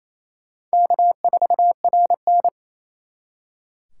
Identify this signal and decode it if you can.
Morse code, heard as a keyed tone, about 34 wpm, 700 Hz. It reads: K4RN
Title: Speed Racing - USA Call Signs 28wpm